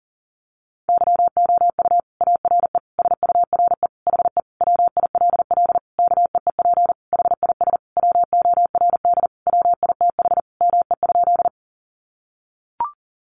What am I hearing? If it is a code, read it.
Morse code, heard as a keyed tone, about 40 wpm, 700 Hz. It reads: YOU ARE SURE HE WILL KEEP HIS WORD WITH ME?
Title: Sentences from Top 300 Words 40wpm